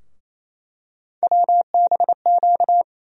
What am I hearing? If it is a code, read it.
Morse code, heard as a keyed tone, about 28 wpm, 700 Hz. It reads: WBQ
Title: USA Call Sign Postfixes - 3 Letters 28wpm